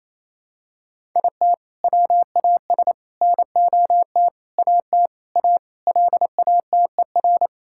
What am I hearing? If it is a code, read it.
Morse code, heard as a keyed tone, about 28 wpm, 700 Hz. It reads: IT WAS NOT AT A LATER
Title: Sentences from Top 1000 Words - Review Part 5 - 28wpm